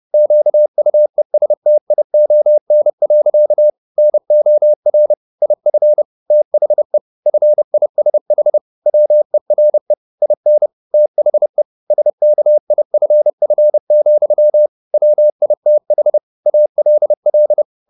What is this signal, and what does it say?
Morse code, heard as a keyed tone, about 30 wpm, 600 Hz. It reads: QUESTION. NOR IF THE FISH WERE IN THE SKIFF, WITH ALL